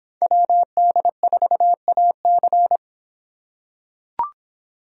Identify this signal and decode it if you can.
Morse code, heard as a keyed tone, about 26 wpm, 700 Hz. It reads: WD4AC